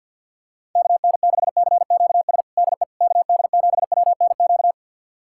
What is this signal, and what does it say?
Morse code, heard as a keyed tone, about 47 wpm, 690 Hz. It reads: KN6CQS DE KD7WNQ
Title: USA Callsign DE Callsign - 6 Characters - 50wpm